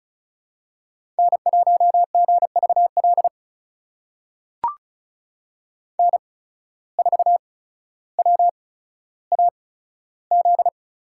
Morse code, heard as a keyed tone, about 36 wpm, spaced as for 15 wpm, 700 Hz. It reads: N1GVL N4WAZ
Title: USA Call Signs - 5 Characters - 8x Character Spacing - 35wpm